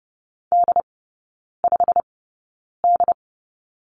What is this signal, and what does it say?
Morse code, heard as a keyed tone, about 30 wpm, spaced as for 9 wpm, 700 Hz. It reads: D5D
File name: course / 30:10wpm / 087 Lesson 15e - Number 5 - Three Characters 30wpm